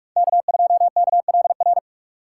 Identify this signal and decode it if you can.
Morse code, heard as a keyed tone, about 48 wpm, 700 Hz. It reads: K2KLR
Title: USA Call Signs - 4x Character Spacing - 45wpm